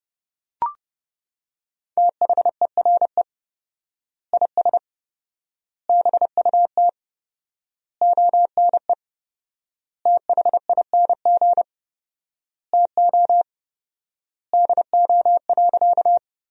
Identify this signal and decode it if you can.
Morse code, heard as a keyed tone, about 30 wpm, 700 Hz. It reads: THERE IS BUT ONE THING TO DO.
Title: 4x Word-Spacing - Sentences from Top 100 Words - 30wpm